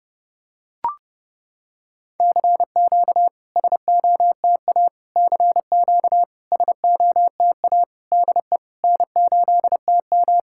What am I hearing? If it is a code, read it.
Morse code, heard as a keyed tone, about 30 wpm, 690 Hz. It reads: CQ SOTA CQ SOTA DE N8TM
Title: SOTA QSO Practice 30wpm